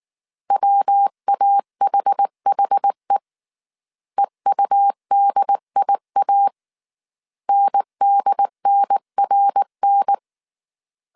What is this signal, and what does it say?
Morse code, heard as a keyed tone, about 19 wpm, 800 Hz. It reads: WAHHE EUDIA NDNRN
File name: CW / DET / 1 3 1500C S B 18wpm 1916